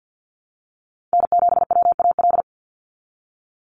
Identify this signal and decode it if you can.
Morse code, heard as a keyed tone, about 50 wpm, 700 Hz. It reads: N7WAL
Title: USA Call Signs 50wpm